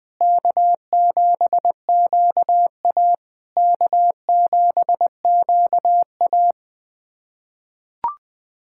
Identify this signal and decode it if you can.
Morse code, heard as a keyed tone, about 20 wpm, 660 Hz. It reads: K7QA K7QA
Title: SOTA QSO Practice 20wpm